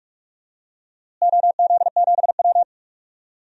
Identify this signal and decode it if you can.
Morse code, heard as a keyed tone, about 41 wpm, 700 Hz. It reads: OZ7W
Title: DXCC-100 Call Signs - 4 Characters - Repeated 2x - 45wpm